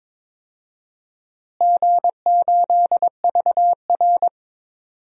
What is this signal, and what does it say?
Morse code, heard as a keyed tone, about 22 wpm, 700 Hz. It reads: G8VR
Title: DXCC-100 Call Signs - 4 Characters - Repeated 2x - 22wpm